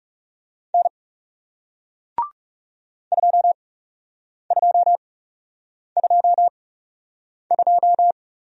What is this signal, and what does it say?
Morse code, heard as a keyed tone, about 40 wpm, spaced as for 8 wpm, 700 Hz. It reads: N 2222